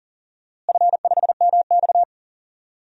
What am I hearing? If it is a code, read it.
Morse code, heard as a keyed tone, about 40 wpm, 700 Hz. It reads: F5MX
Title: DXCC-100 Call Signs - 4 Characters - Repeated 3x - 40wpm